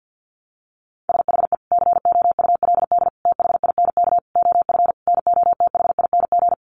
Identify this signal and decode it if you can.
Morse code, heard as a keyed tone, about 50 wpm, 700 Hz. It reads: SHE COULD THINK OF NOTHING
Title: Sentences from Top 800 Words 50wpm